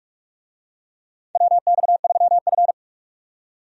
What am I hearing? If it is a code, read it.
Morse code, heard as a keyed tone, about 47 wpm, 700 Hz. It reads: WX3F
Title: USA Call Signs - 4 Characters - 45wpm